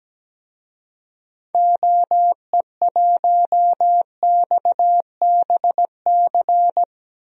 Morse code, heard as a keyed tone, about 17 wpm, 700 Hz. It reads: OE1XBC
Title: DXCC-100 Call Signs - 6 Characters - 17wpm